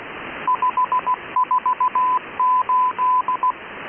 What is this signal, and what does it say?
Morse code, heard as a keyed tone, about 16 wpm, 1010 Hz. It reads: 548